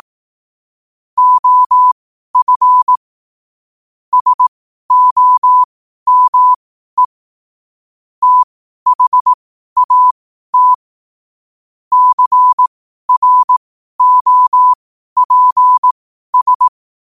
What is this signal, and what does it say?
Morse code, heard as a keyed tone, about 18 wpm, spaced as for 12 wpm, 1000 Hz. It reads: OF SOME THAT CROPS
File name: qotdfiles / 12wpm